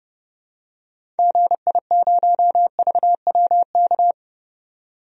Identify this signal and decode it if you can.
Morse code, heard as a keyed tone, about 30 wpm, 700 Hz. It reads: GI0VWK
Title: DXCC-100 Call Signs - 6 Characters - 30wpm